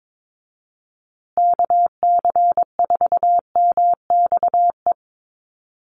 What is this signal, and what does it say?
Morse code, heard as a keyed tone, about 22 wpm, 700 Hz. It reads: KC4MXE
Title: USA Call Signs Repeated 3x 22wpm